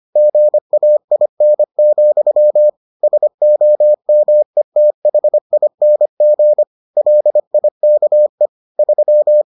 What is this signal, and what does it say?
Morse code, heard as a keyed tone, about 25 wpm, 600 Hz. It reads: GAIN, SOMETHING LIKE 3